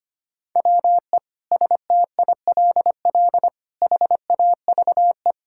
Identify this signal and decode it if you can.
Morse code, heard as a keyed tone, about 25 wpm, 700 Hz. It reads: WE STILL HAVE
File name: practice / 25wpm / Sentences with 9-Word Phrases - 25wpm